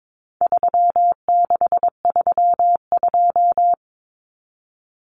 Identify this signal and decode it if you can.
Morse code, heard as a keyed tone, about 22 wpm, 700 Hz. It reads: 3632
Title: Common Cut Numbers - 4-Digit 22wpm